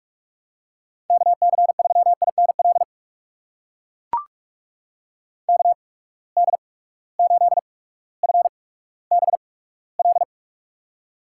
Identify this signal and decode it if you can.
Morse code, heard as a keyed tone, about 44 wpm, spaced as for 19 wpm, 700 Hz. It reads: KC3INL KD8FBL
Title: USA Call Signs - 6 Characters - 8x Character Spacing - 45wpm